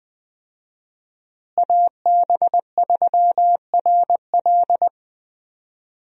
Morse code, heard as a keyed tone, about 21 wpm, 700 Hz. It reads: AB3RL
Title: USA Call Signs - 5 Characters - Repeated 3x - 20wpm